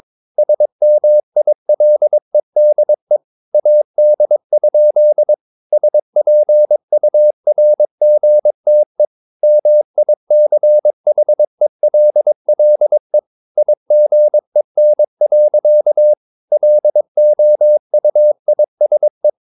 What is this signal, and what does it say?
Morse code, heard as a keyed tone, about 22 wpm, 600 Hz. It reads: SMILEDE AD? SPURGTE MICHELLE IGEN. LOUISE
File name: Ferie QRS 22wpm